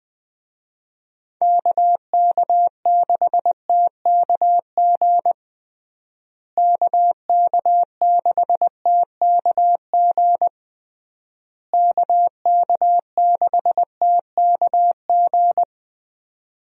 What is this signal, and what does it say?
Morse code, heard as a keyed tone, about 20 wpm, 700 Hz. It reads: KK6TKG KK6TKG KK6TKG
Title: USA Call Signs - 6 Characters - Repeated 3x - 20wpm